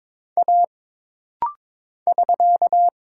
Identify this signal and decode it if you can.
Morse code, heard as a keyed tone, about 22 wpm, 700 Hz. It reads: A <SK>
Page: This rapid-fire timing is geared to push your ICR (Instant Character Recognition) skills.